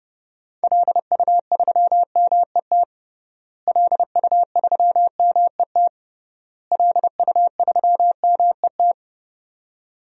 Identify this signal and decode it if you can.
Morse code, heard as a keyed tone, about 31 wpm, 700 Hz. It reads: LU3MET LU3MET LU3MET
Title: DXCC-100 Call Signs - 6 Characters - Repeated 3x - 30wpm